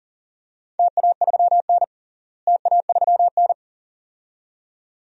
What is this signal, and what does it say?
Morse code, heard as a keyed tone, about 42 wpm, 690 Hz. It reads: TA3N TA3N
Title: DXCC-100 Call Signs - 4 Characters - Repeated 2x - 40wpm